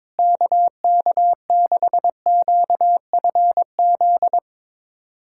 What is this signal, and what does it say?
Morse code, heard as a keyed tone, about 22 wpm, 700 Hz. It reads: KK6QFZ
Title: USA Callsign DE Callsign 22wpm